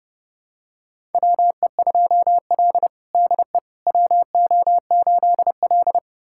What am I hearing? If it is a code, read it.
Morse code, heard as a keyed tone, about 32 wpm, 700 Hz. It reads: WE2L DE WO8L
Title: USA Callsign DE Callsign - 4 Characters - 30wpm